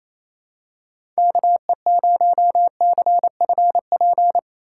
Morse code, heard as a keyed tone, about 28 wpm, 700 Hz. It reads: KE0CFP
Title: USA Call Signs - 6 Characters - 3x Character Spacing - 28wpm